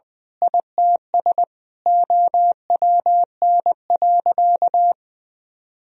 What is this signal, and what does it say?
Morse code, heard as a keyed tone, about 20 wpm, 700 Hz. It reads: ITS OWN.
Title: Sentences from Top 1000 Words - Review Part 4 - 20wpm